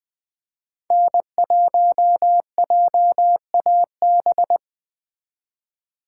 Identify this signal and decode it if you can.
Morse code, heard as a keyed tone, about 20 wpm, 700 Hz. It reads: N1JAB
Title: Speed Racing - USA Call Signs 17wpm